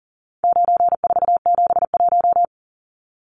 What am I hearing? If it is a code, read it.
Morse code, heard as a keyed tone, about 40 wpm, 700 Hz. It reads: 9471
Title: Numbers - 4-Digit 40wpm